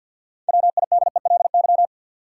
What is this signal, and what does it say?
Morse code, heard as a keyed tone, about 50 wpm, 700 Hz. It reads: WIDELY
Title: Sentences from Top 1200 Words - New Word Drill - 50wpm